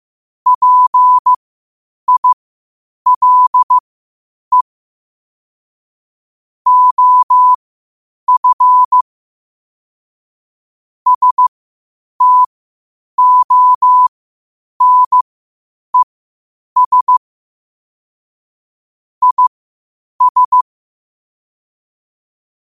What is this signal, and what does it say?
Morse code, heard as a keyed tone, about 15 wpm, spaced as for 8 wpm, 1000 Hz. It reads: PILE OF STONES IS